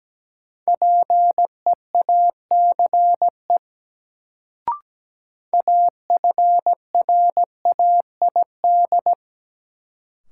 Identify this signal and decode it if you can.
Morse code, heard as a keyed tone, about 17 wpm, 700 Hz. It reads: PEACE AFRAID
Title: Sentences from Top 1500 Words - New Word Drill - 17wpm